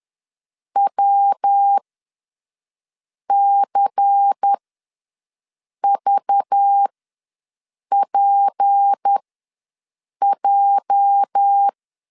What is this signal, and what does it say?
Morse code, heard as a keyed tone, about 10 wpm, spaced as for 5 wpm, 800 Hz. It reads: WCVPJ
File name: LEZIONI / 8 YJ45 / 8 YJ4 ws 10wpm 4020